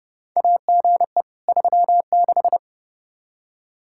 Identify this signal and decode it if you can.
Morse code, heard as a keyed tone, about 30 wpm, 700 Hz. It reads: AGE 36
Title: Practice QSO - Exchange 3 - More Info - 30wpm